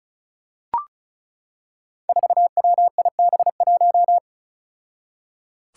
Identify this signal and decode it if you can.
Morse code, heard as a keyed tone, about 36 wpm, 700 Hz. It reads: 4WIB1
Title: Five Challenging Letter-Number Combinations 35wpm